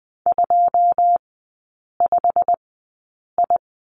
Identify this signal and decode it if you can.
Morse code, heard as a keyed tone, about 20 wpm, spaced as for 8 wpm, 690 Hz. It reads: 25I
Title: 087 Lesson 15e - Number 5 - Three Characters 20wpm